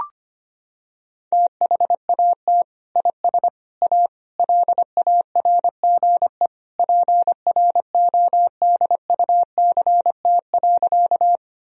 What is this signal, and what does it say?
Morse code, heard as a keyed tone, about 25 wpm, 700 Hz. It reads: THAT IS A LARGE PRODUCT.
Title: Sentences from Top 1000 Words - Review Part 2 - 25wpm